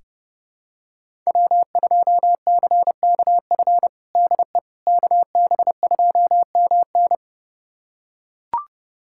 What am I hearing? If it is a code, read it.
Morse code, heard as a keyed tone, about 30 wpm, 700 Hz. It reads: W2CKF DE KB2MN
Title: USA Callsign DE Callsign - 5 Characters - 30wpm